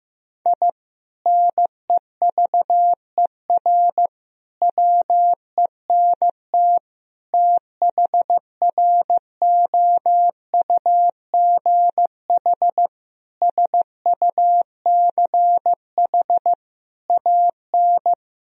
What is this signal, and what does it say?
Morse code, heard as a keyed tone, about 15 wpm, 700 Hz. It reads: I NEVER WENT THROUGH SUCH AN
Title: Sentences from Top 1000 Words - Review Part 3 - 15wpm